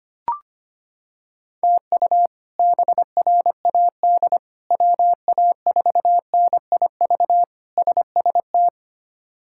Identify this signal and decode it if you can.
Morse code, heard as a keyed tone, about 25 wpm, 700 Hz. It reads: TU BRAD WA4NIV SST
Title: K1USN SST QSO Practice - 25wpm